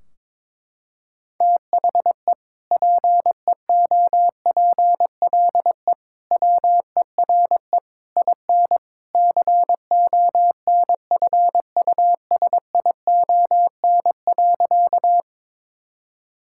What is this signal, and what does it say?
Morse code, heard as a keyed tone, about 22 wpm, 700 Hz. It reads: THE PEOPLE WERE IN CONFUSION.
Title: Sentences from Top 1600 Words 22wpm